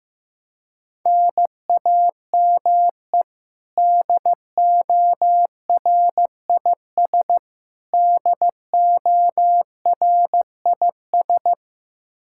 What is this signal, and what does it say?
Morse code, heard as a keyed tone, about 15 wpm, 700 Hz. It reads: NAME DORIS DORIS
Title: Practice QSO - Exchange 1 - Beginning - 15wpm